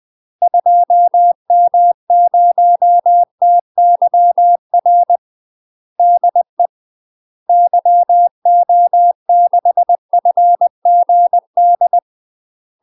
Morse code, heard as a keyed tone, about 20 wpm, 700 Hz. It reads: 2M0TYR DE YO6FGD
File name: UK MASTER QSOs TRAINING / 20WPM / UK MASTER QSOs 0010